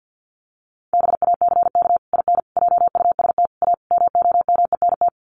Morse code, heard as a keyed tone, about 50 wpm, 700 Hz. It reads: BACK IN JUST A MOMENT
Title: Sets of 5 Words Encore 50wpm